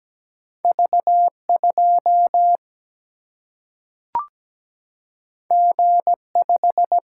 Morse code, heard as a keyed tone, about 17 wpm, 700 Hz. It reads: V2 G5
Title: Single-Letter Single-Number 17wpm